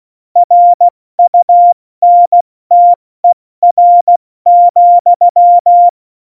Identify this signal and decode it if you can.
Morse code, heard as a keyed tone, about 16 wpm, 700 Hz. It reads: RUNTER,
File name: Morsekurs / Witze / Witz 14. November 2017, 70 F80